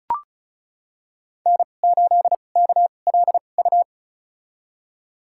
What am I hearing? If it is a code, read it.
Morse code, heard as a keyed tone, about 35 wpm, spaced as for 25 wpm, 690 Hz. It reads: N8KLU
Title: USA Call Signs - 5 Characters - 2x Character Spacing - 35wpm